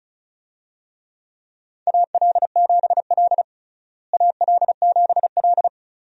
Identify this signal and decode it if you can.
Morse code, heard as a keyed tone, about 36 wpm, 700 Hz. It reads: AL7L AL7L
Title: USA Call Signs - 4 Characters - Repeated 3x - 35wpm